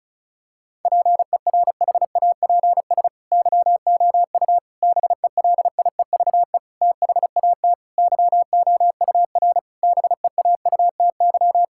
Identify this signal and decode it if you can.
Morse code, heard as a keyed tone, about 34 wpm, 700 Hz. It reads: PERHAPS YOU BELIEVE THAT YOUR BEAUTY
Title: Sentences from Top 1600 Words 35wpm